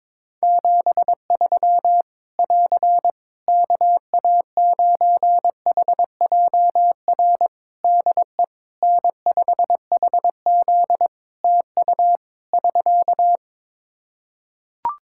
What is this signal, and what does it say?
Morse code, heard as a keyed tone, about 22 wpm, 700 Hz. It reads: 73 <AR> KA9HJR DE N5HZ TU <SK>
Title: Practice QSO - Exchange 4 - Ending - 22wpm